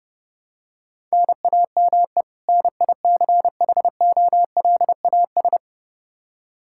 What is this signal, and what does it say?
Morse code, heard as a keyed tone, about 30 wpm, 700 Hz. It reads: NAME NICHOLAS
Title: K1USN SST QSO Practice 30wpm